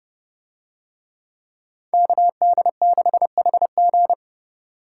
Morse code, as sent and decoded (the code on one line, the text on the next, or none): -.- -.. -.... .... --.
KD6HG